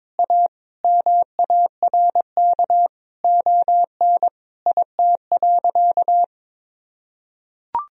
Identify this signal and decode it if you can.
Morse code, heard as a keyed tone, about 22 wpm, 700 Hz. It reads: A MARK ON IT.
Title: Sentences from Top 1000 Words - Review Part 1 - 22wpm